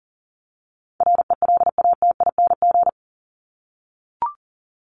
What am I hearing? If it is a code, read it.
Morse code, heard as a keyed tone, about 40 wpm, 700 Hz. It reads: RELATING
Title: ING Suffix - Top 100 Words with Base Words 40wpm